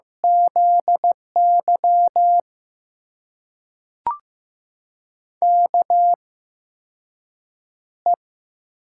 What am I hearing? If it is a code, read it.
Morse code, heard as a keyed tone, about 15 wpm, spaced as for 6 wpm, 700 Hz. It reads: ZY KE
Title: USA Call Signs - 6 Characters - 8x Character Spacing - 15wpm